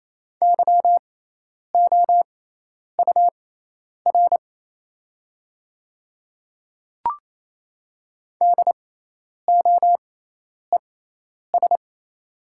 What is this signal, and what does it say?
Morse code, heard as a keyed tone, about 28 wpm, spaced as for 10 wpm, 700 Hz. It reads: YOUR DOES
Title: 6x Character Spacing - Top 200 Words - 28wpm